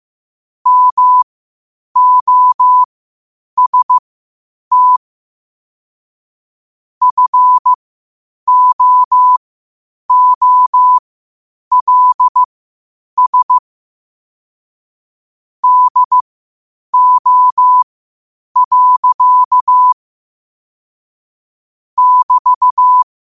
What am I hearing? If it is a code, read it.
Morse code, heard as a keyed tone, about 15 wpm, spaced as for 8 wpm, 1000 Hz. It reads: MOST FOOLS DO. <BT>